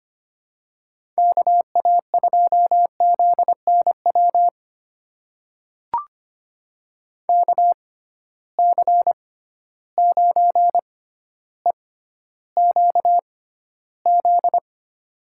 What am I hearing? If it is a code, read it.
Morse code, heard as a keyed tone, about 25 wpm, spaced as for 13 wpm, 700 Hz. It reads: KA2ZNW KC9EQZ
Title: USA Call Signs - 6 Characters - 6x Character Spacing - 25wpm